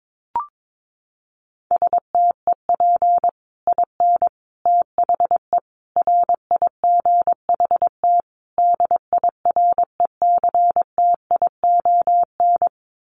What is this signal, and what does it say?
Morse code, heard as a keyed tone, about 22 wpm, 700 Hz. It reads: STEP IN THE RIGHT DIRECTION
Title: Sets of 5 Words Encore 22wpm